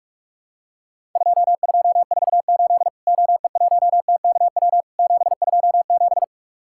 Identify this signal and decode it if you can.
Morse code, heard as a keyed tone, about 46 wpm, 700 Hz. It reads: 2248 OE1TKW 727
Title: Worked All Europe DX Contest - QTC Exchange - 45wpm